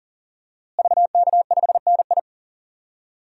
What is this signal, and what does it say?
Morse code, heard as a keyed tone, about 40 wpm, 700 Hz. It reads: VK5NI
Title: DXCC-100 Call Signs - 5 Characters - Repeated 2x - 40wpm